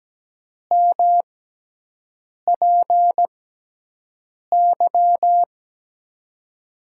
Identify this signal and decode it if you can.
Morse code, heard as a keyed tone, about 17 wpm, spaced as for 6 wpm, 700 Hz. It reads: MPY
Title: USA Call Signs - 6 Characters - 6x Character Spacing - 17wpm